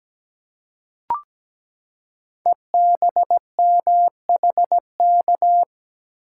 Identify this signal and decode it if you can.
Morse code, heard as a keyed tone, about 17 wpm, 700 Hz. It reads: EBMHK
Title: Code Groups - 5 Letters - 17wpm